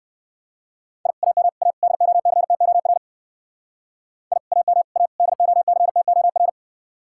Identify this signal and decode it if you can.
Morse code, heard as a keyed tone, about 58 wpm, 700 Hz. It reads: I AM A DOCTOR I AM A DOCTOR